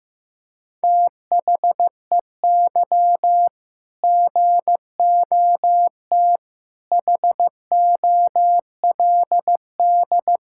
Morse code, heard as a keyed tone, about 16 wpm, 700 Hz. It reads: THEY GOT HOLD
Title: Sentences from Top 1600 Words 15wpm